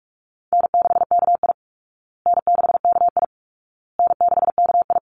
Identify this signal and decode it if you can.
Morse code, heard as a keyed tone, about 44 wpm, 700 Hz. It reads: N6KI N6KI N6KI
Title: Easy Call Signs Repeated 3x - 45wpm